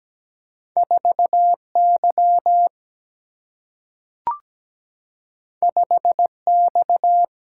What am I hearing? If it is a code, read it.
Morse code, heard as a keyed tone, about 17 wpm, 700 Hz. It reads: 4Y 5X
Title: Single-Letter Single-Number 17wpm